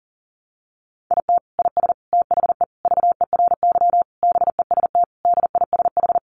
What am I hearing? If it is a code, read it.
Morse code, heard as a keyed tone, about 40 wpm, 700 Hz. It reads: IT IS THE VERY BEST DISH
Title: Sentences from Top 1300 Words 40wpm